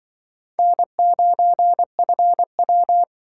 Morse code, heard as a keyed tone, about 25 wpm, 700 Hz. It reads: N9FW
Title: Speed Racing - USA Call Signs 20wpm